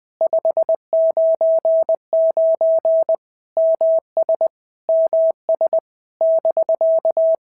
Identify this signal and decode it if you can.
Morse code, heard as a keyed tone, about 20 wpm, 650 Hz. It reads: 599 MS MS <BK>
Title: POTA QSO Practice - 20wpm